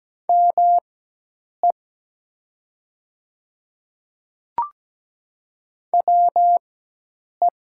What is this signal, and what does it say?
Morse code, heard as a keyed tone, about 17 wpm, spaced as for 8 wpm, 700 Hz. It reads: ME WE
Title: 4x Character Spacing - Top 200 Words - 17wpm